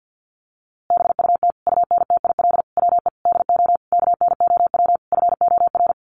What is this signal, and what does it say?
Morse code, heard as a keyed tone, about 50 wpm, 700 Hz. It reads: BUT UNTIL WE DO KNOW FOR
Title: A Sound Of Thunder 50wpm